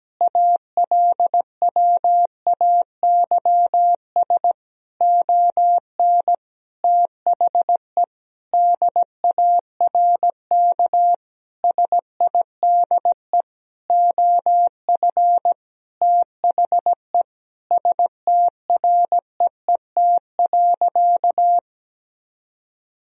Morse code, heard as a keyed tone, about 17 wpm, 700 Hz. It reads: ALWAYS ON THE DARK SIDE OF THE STREET.
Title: Sentences from Top 1000 Words - Review Part 3 - 17wpm